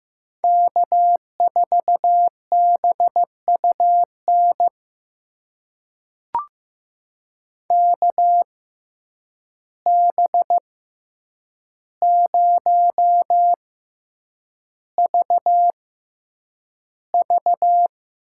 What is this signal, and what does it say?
Morse code, heard as a keyed tone, about 15 wpm, spaced as for 8 wpm, 700 Hz. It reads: K4BUN KB0VV
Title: USA Call Signs - 5 Characters - 6x Character Spacing - 15wpm